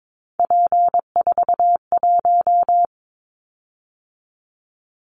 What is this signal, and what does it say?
Morse code, heard as a keyed tone, about 22 wpm, 700 Hz. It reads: P41
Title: Callsign Prefixes for Top 100 DXCC - 22wpm